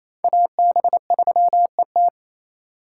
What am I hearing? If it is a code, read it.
Morse code, heard as a keyed tone, about 28 wpm, 700 Hz. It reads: AB3ET
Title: USA Call Signs - 5 Characters - 2x Character Spacing - 28wpm